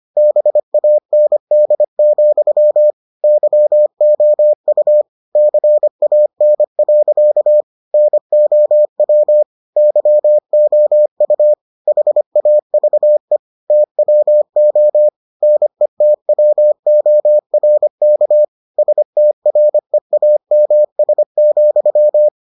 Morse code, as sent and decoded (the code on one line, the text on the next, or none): -... .- -. -.. --..-- / -.-- --- ..- / -.-. .- -. .-.-.- / -. --- .-- / -.-- --- ..- / .... .- ...- . / - .-- --- / -. - .-- --- .-. -.- / ... - .-. .- -- ... --..--
BAND, YOU CAN. NOW YOU HAVE TWO NETWORK STREAMS,